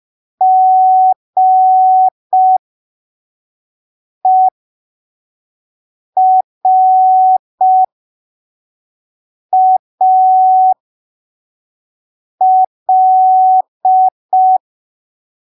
Morse code, heard as a keyed tone, about 5 wpm, spaced as for 3 wpm, 750 Hz. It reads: GERAL